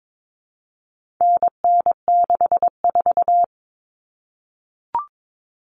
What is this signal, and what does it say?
Morse code, heard as a keyed tone, about 22 wpm, 700 Hz. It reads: NN64
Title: Common Cut Numbers - 4-Digit 22wpm